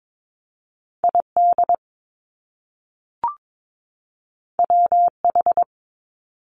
Morse code, heard as a keyed tone, about 22 wpm, 700 Hz. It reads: ID WH
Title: Top Two Letter Combinations in Words 22wpm